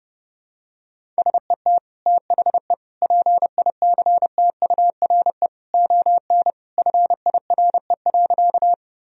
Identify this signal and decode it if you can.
Morse code, heard as a keyed tone, about 30 wpm, 700 Hz. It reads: SET THE PICTURE ON FIRE.
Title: Sentences from Top 1000 Words - Review Part 1 - 30wpm